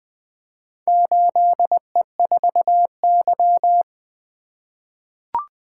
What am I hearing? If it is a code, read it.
Morse code, heard as a keyed tone, about 20 wpm, 700 Hz. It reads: 8E4Y
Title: Four Letter-Number Combinations - 20wpm